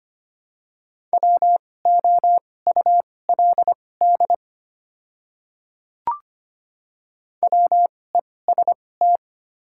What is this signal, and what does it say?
Morse code, heard as a keyed tone, about 25 wpm, spaced as for 18 wpm, 690 Hz. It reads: WOULD WEST
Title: 2x Character Spacing - Top 500 Words - 25wpm